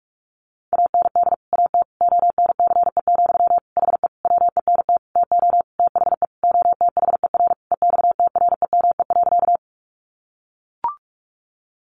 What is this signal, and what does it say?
Morse code, heard as a keyed tone, about 44 wpm, 700 Hz. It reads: AND AT ONCE, HE WENT TO THE OTHER EXTREME.
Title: Sentences from Top 1300 Words 45wpm